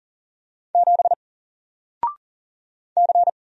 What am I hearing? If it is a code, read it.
Morse code, heard as a keyed tone, about 40 wpm, 700 Hz. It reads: 7 C
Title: Single Character (Letter:Number:Punctuation:Prosign) 40wpm